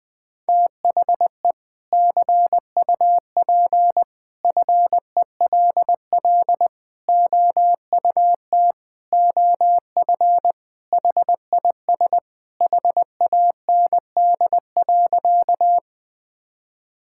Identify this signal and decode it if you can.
Morse code, heard as a keyed tone, about 20 wpm, 700 Hz. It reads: THE CUP FELL OUT OF HIS HAND.
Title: Sentences from Top 1600 Words 20wpm